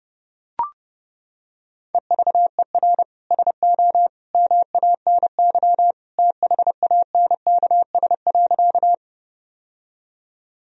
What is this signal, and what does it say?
Morse code, heard as a keyed tone, about 30 wpm, 700 Hz. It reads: EVER SO MANY THANKS.
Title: Sentences from Top 1000 Words - Review Part 3 - 30wpm